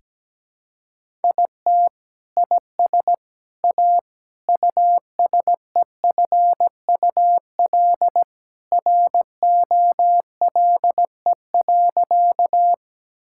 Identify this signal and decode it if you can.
Morse code, heard as a keyed tone, about 17 wpm, 700 Hz. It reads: IT IS A USEFUL ROLE.
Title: Sentences from Top 1000 Words - Review Part 6 - 17wpm